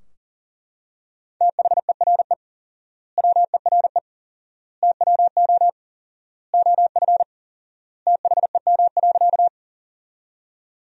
Morse code, heard as a keyed tone, about 40 wpm, 700 Hz. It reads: THERE WERE TWO OF THEM.
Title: 4x Word-Spacing - Sentences from Top 100 Words - 40wpm